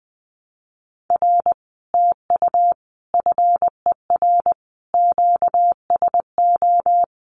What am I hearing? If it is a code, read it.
Morse code, heard as a keyed tone, about 21 wpm, 700 Hz. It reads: R TU FER QSO
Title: Common QSO Phrases Encore 20wpm